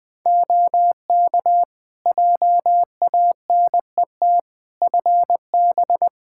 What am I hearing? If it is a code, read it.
Morse code, heard as a keyed tone, about 20 wpm, 700 Hz. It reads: OK JANET FB
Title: Practice QSO - Exchange 2 - Info - 20wpm